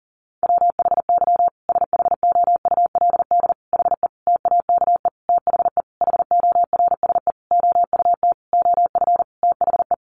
Morse code, heard as a keyed tone, about 40 wpm, 700 Hz. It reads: WHY SHOULD HE TAKE THE HORSE OUT OF THE
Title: Sentences from Top 1500 Words 40wpm